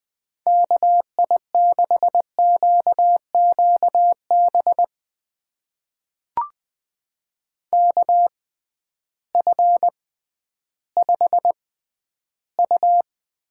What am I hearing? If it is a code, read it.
Morse code, heard as a keyed tone, about 20 wpm, spaced as for 12 wpm, 700 Hz. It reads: KI6QQB KF5U
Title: USA Call Signs - 6 Characters - 6x Character Spacing - 20wpm